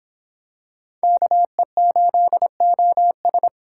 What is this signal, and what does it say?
Morse code, heard as a keyed tone, about 26 wpm, 700 Hz. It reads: KE8OS